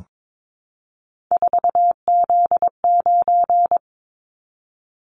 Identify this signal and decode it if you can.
Morse code, heard as a keyed tone, about 22 wpm, 700 Hz. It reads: 4Z9
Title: Callsign Prefixes for Top 50 DXCC - 22wpm